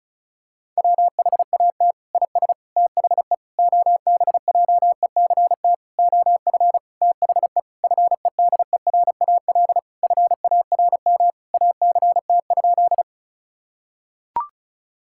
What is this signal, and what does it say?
Morse code, heard as a keyed tone, about 36 wpm, 700 Hz. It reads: WHAT IS THE OBJECT OF THE FEDERAL FARM ACT?
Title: Sentences from Top 1000 Words - Review Part 5 - 35wpm